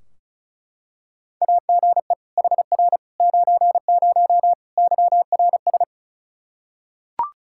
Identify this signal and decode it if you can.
Morse code, heard as a keyed tone, about 34 wpm, 700 Hz. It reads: AGE HR 90 YRS
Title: Practice QSO - Exchange 3 - More Info - 35wpm